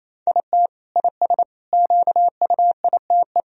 Morse code, heard as a keyed tone, about 28 wpm, 700 Hz. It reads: IT IS QUITE
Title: Sentences from Top 1000 Words - Review Part 4 - 28wpm